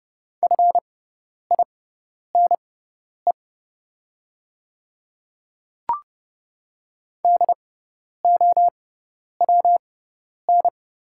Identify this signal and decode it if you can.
Morse code, heard as a keyed tone, about 30 wpm, spaced as for 10 wpm, 700 Hz. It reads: FINE DOWN
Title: 6x Character Spacing - Top 500 Words - 30wpm